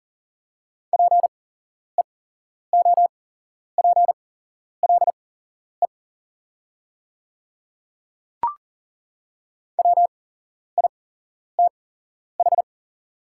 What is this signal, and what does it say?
Morse code, heard as a keyed tone, about 40 wpm, spaced as for 11 wpm, 700 Hz. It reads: PEOPLE WITH
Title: 8x Character Spacing - Top 100 Words - 40wpm